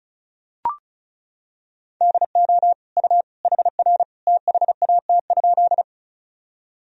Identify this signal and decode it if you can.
Morse code, heard as a keyed tone, about 35 wpm, 700 Hz. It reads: DO U HR THAT?
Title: Sentences from Top 300 Words and Abbreviations 35wpm